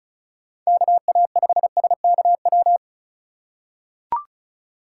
Morse code, heard as a keyed tone, about 34 wpm, 700 Hz. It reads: KA5SKW
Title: USA Call Signs - 6 Characters - 3x Character Spacing - 35wpm